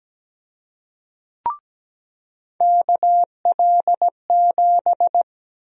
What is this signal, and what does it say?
Morse code, heard as a keyed tone, about 17 wpm, 700 Hz. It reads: KL7
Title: USA Call Sign Prefixes - 17wpm